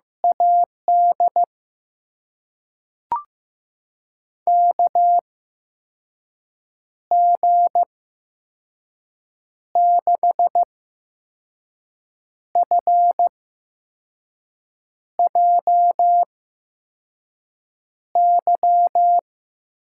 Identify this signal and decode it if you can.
Morse code, heard as a keyed tone, about 15 wpm, spaced as for 5 wpm, 700 Hz. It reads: AD KG6FJY
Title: USA Call Signs - 6 Characters - 8x Character Spacing - 15wpm